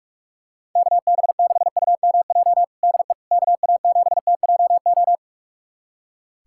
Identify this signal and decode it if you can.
Morse code, heard as a keyed tone, about 46 wpm, 700 Hz. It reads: KB6UMJ DE KA7TJO
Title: USA Callsign DE Callsign - 6 Characters - 45wpm